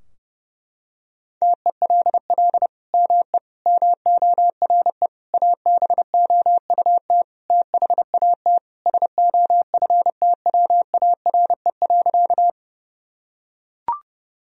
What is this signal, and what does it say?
Morse code, heard as a keyed tone, about 30 wpm, 700 Hz. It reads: TELL ME MORE ABOUT THAT SOFTWARE.
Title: Sentences from Top 1000 Words - Review Part 3 - 30wpm